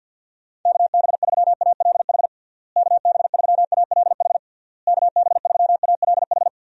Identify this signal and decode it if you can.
Morse code, heard as a keyed tone, about 51 wpm, 700 Hz. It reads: KB3ALH KB3ALH KB3ALH
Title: USA Call Signs - 6 Characters - Repeated 3x - 50wpm